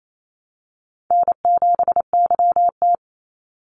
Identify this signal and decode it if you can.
Morse code, heard as a keyed tone, about 28 wpm, 700 Hz. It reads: N7YT
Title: Call Signs Repeated 3x - 28wpm